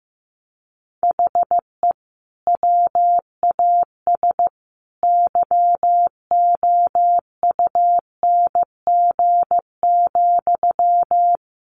Morse code, heard as a keyed tone, about 15 wpm, 700 Hz. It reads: HE WAS YOUNG,
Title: Sentences from Top 1100 Words 15wpm